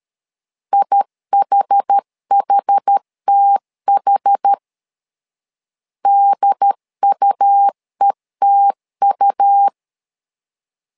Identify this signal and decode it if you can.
Morse code, heard as a keyed tone, about 13 wpm, 800 Hz. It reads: IHHTH DUETU
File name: LEZIONI / 2 UHDN / 2 UHD 12wpm 1540